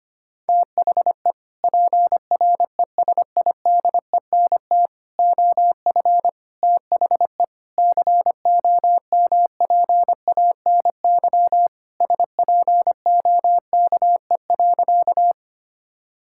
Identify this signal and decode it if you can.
Morse code, heard as a keyed tone, about 25 wpm, 700 Hz. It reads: THE PRESIDENT OF THE COMPANY SPOKE.
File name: practice / 25wpm / Sentences with 3-Word Phrases - 25wpm